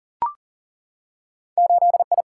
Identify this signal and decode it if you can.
Morse code, heard as a keyed tone, about 38 wpm, 700 Hz. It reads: 8I
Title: Single-Letter Single-Number 40wpm